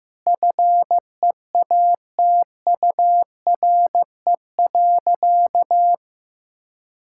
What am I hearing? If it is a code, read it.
Morse code, heard as a keyed tone, about 15 wpm, 700 Hz. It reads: FEATURE.
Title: Sentences from Top 1000 Words - Review Part 5 - 15wpm